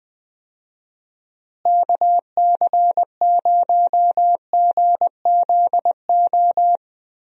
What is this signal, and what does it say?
Morse code, heard as a keyed tone, about 20 wpm, 700 Hz. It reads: KC0GZO
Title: USA Call Signs - 6 Characters - Repeated 2x - 20wpm